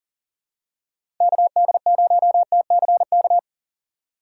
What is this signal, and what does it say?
Morse code, heard as a keyed tone, about 40 wpm, 700 Hz. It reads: KD0TCK
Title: USA Call Signs - 6x Character Spacing - 40wpm